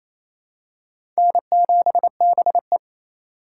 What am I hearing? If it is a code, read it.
Morse code, heard as a keyed tone, about 28 wpm, 700 Hz. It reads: N7BE
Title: USA Call Signs - 4 Characters - 3x Character Spacing - 28wpm